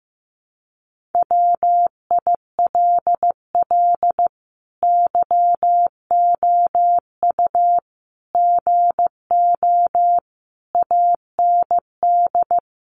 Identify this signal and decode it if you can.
Morse code, heard as a keyed tone, about 15 wpm, 700 Hz. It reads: WILL YOU GO AND
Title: Sentences from Top 1300 Words 15wpm